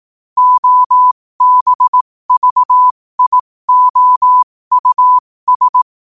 Morse code, heard as a keyed tone, about 18 wpm, spaced as for 15 wpm, 1000 Hz. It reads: OBVIOUS
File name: qotdfiles / 15wpm